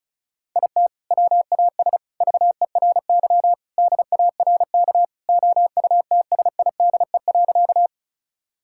Morse code, heard as a keyed tone, about 36 wpm, 700 Hz. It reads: IT WAS VERY DARK OUTSIDE.
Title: Sentences from Top 1000 Words - Review Part 6 - 35wpm